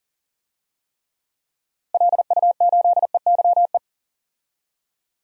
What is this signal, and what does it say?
Morse code, heard as a keyed tone, about 40 wpm, 700 Hz. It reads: LU8EYE
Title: DXCC-100 Call Signs - 6 Characters - Repeated 2x - 40wpm